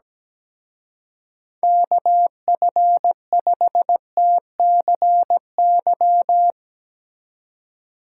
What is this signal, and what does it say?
Morse code, heard as a keyed tone, about 19 wpm, 700 Hz. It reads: KF5TCY
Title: Speed Racing - USA 2x3 Call Signs 17wpm